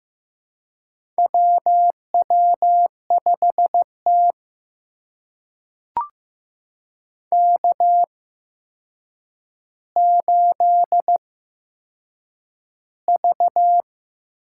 Morse code, heard as a keyed tone, about 15 wpm, spaced as for 7 wpm, 700 Hz. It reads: WW5T K8V
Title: USA Call Signs - 4 Characters - 8x Character Spacing - 15wpm